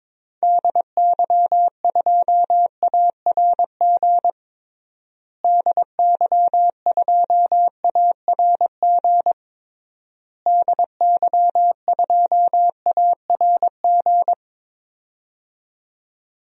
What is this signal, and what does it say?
Morse code, heard as a keyed tone, about 21 wpm, 700 Hz. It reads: DY2ARG DY2ARG DY2ARG
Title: DXCC-100 Call Signs - 6 Characters - Repeated 3x - 22wpm